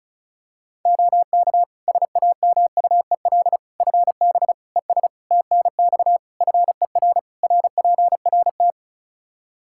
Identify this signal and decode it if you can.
Morse code, heard as a keyed tone, about 36 wpm, 700 Hz. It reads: OK SAMUEL FB ES TNX FER RPRT
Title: Practice QSO - Exchange 2 - Info - 35wpm